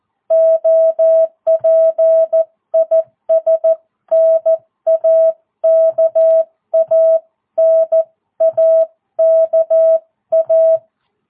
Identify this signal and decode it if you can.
Morse code, heard as a keyed tone, about 14 wpm, 650 Hz. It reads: OP IS NAKA NAKA
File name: jh8cbh / 2022cw